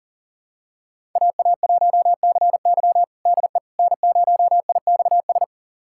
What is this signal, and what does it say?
Morse code, heard as a keyed tone, about 40 wpm, 700 Hz. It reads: AA1CY DE N0IXS
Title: USA Callsign DE Callsign - 5 Characters - 40wpm